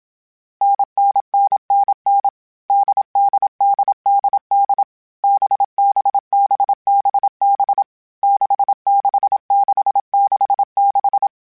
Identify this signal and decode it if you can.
Morse code, heard as a keyed tone, about 27 wpm, 800 Hz. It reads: NNNNN DDDDD BBBBB 66666
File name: Variable Speed / Variable Speed-mp3s / GZ GZ7 sp @25